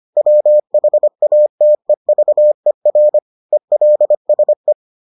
Morse code, heard as a keyed tone, about 25 wpm, 600 Hz. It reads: WHATEVER ELSE